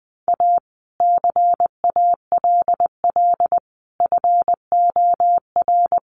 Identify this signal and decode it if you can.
Morse code, heard as a keyed tone, about 20 wpm, 700 Hz. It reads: A CALL FOR
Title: Sentences from Top 300 Words 20wpm